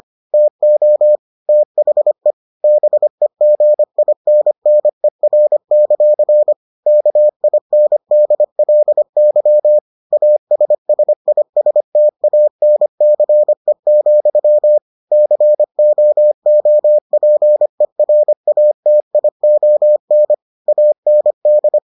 Morse code, heard as a keyed tone, about 25 wpm, 600 Hz. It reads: TO THE BEGINNER; KINDLY ASSISTANCE, COOPERATION AND